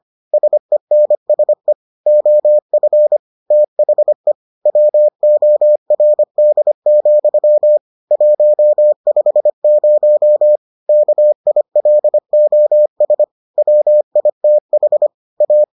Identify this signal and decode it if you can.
Morse code, heard as a keyed tone, about 25 wpm, 600 Hz. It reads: SENSE OF THE WORD, 150 KILOS WITH A